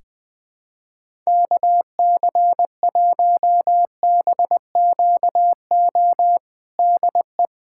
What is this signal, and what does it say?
Morse code, heard as a keyed tone, about 20 wpm, 700 Hz. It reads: KC1BQO DE
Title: USA Callsign DE Callsign - 6 Characters - 20wpm